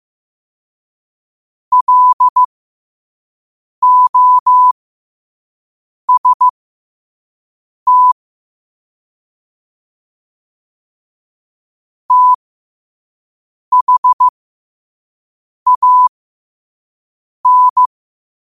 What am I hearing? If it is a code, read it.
Morse code, heard as a keyed tone, about 15 wpm, spaced as for 5 wpm, 1000 Hz. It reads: LOST THAN